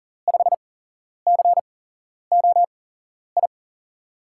Morse code, heard as a keyed tone, about 39 wpm, spaced as for 11 wpm, 700 Hz. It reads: 5COI
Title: USA Call Signs - 6 Characters - 8x Character Spacing - 40wpm